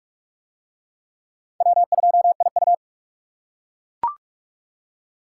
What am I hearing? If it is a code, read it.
Morse code, heard as a keyed tone, about 41 wpm, 690 Hz. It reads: W2IU